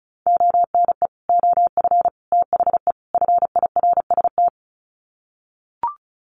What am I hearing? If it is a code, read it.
Morse code, heard as a keyed tone, about 34 wpm, 700 Hz. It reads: ONE OF THE FIRST
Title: Sets of 4 Words 35wpm